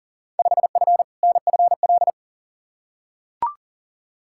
Morse code, heard as a keyed tone, about 40 wpm, 700 Hz. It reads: 5F NFL
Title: Field Day Exchanges Encore - 40wpm